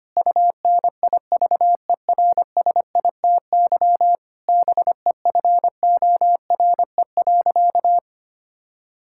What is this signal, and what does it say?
Morse code, heard as a keyed tone, about 25 wpm, 700 Hz. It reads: UNIVERSITY BEFORE.
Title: Sentences from Top 1000 Words - Review Part 5 - 25wpm